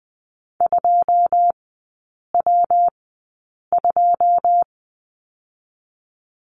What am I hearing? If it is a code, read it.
Morse code, heard as a keyed tone, about 20 wpm, spaced as for 8 wpm, 700 Hz. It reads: 2W2
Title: 114 Lesson 19e - Letter w - Three Characters 20wpm